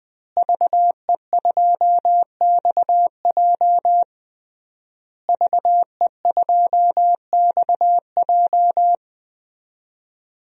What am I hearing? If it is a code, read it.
Morse code, heard as a keyed tone, about 21 wpm, 700 Hz. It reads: VE2XJ VE2XJ
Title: DXCC-100 Call Signs - 5 Characters - Repeated 3x - 20wpm